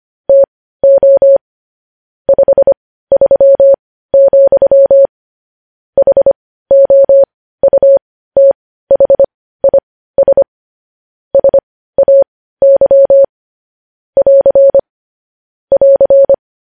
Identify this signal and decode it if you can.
Morse code, heard as a keyed tone, about 25 wpm, spaced as for 15 wpm, 550 Hz. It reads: TO 53, HOUTHIS SAY <AR> <AR>
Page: news headlines in morse code at 15 wpm